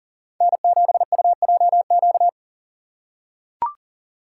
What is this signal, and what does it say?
Morse code, heard as a keyed tone, about 39 wpm, 700 Hz. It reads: N7UJQ
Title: USA Call Signs - 5 Characters - 3x Character Spacing - 40wpm